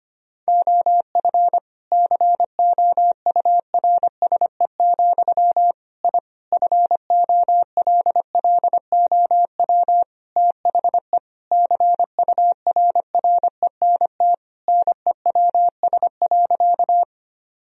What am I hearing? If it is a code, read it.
Morse code, heard as a keyed tone, about 25 wpm, 700 Hz. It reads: OF COURSE, I FOLLOW THE CURRENT NEWS.
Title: Sentences from Top 1000 Words - Review Part 4 - 25wpm